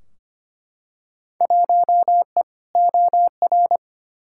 Morse code, heard as a keyed tone, about 25 wpm, 700 Hz. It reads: 1E OR
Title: Field Day Exchanges Encore - 25wpm